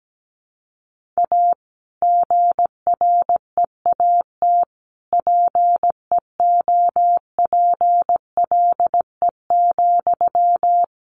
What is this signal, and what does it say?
Morse code, heard as a keyed tone, about 17 wpm, 700 Hz. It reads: A GREAT PEOPLE,
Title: Sentences from Top 1500 Words 17wpm